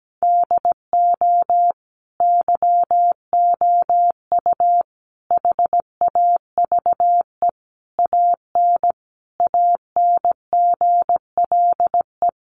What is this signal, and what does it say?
Morse code, heard as a keyed tone, about 17 wpm, 700 Hz. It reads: DO YOU HAVE AN ANGLE
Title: Sentences from Top 1500 Words 17wpm